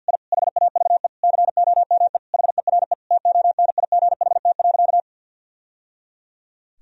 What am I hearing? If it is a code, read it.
Morse code, heard as a keyed tone, about 46 wpm, 700 Hz. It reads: I HAVE COME HERE TONIGHT.
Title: Sentences from Top 1000 Words - Review Part 3 - 50wpm